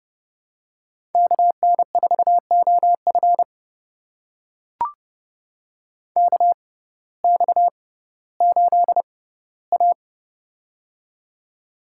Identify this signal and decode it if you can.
Morse code, heard as a keyed tone, about 30 wpm, spaced as for 17 wpm, 700 Hz. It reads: KN4OF KX8A
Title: USA Call Signs - 6x Character Spacing - 30wpm